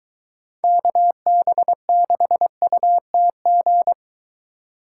Morse code, heard as a keyed tone, about 23 wpm, 700 Hz. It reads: KB6UTG